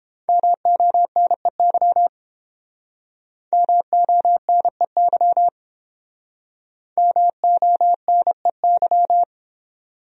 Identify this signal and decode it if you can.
Morse code, heard as a keyed tone, about 29 wpm, 700 Hz. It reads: MONEY MONEY MONEY